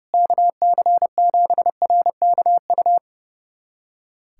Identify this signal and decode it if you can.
Morse code, heard as a keyed tone, about 30 wpm, 700 Hz. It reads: KC7RKU
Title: USA Callsign DE Callsign 30wpm